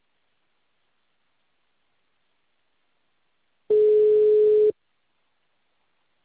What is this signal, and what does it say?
Ringback tone (Europe, 425 Hz).
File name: out- -2011- - - .41866